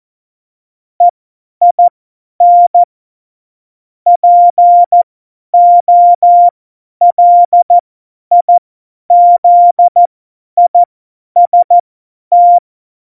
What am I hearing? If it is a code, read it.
Morse code, heard as a keyed tone, about 14 wpm, spaced as for 10 wpm, 700 Hz. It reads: EIN POLIZIST
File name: Morsekurs / Witze / Witz 05. Februar 2019, 50 F70